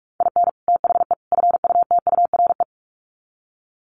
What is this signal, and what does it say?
Morse code, heard as a keyed tone, about 44 wpm, 700 Hz. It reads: IN THE FUTURE
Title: Sets of 3 Words 45wpm